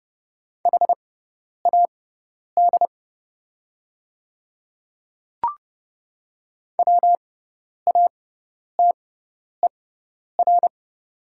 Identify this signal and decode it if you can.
Morse code, heard as a keyed tone, about 30 wpm, spaced as for 10 wpm, 700 Hz. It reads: HAD WATER
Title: 6x Character Spacing - Top 500 Words - 30wpm